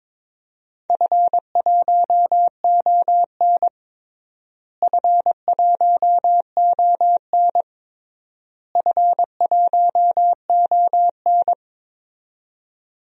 Morse code, heard as a keyed tone, about 22 wpm, 700 Hz. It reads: F1ON F1ON F1ON
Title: DXCC-100 Call Signs - 4 Characters - Repeated 3x - 22wpm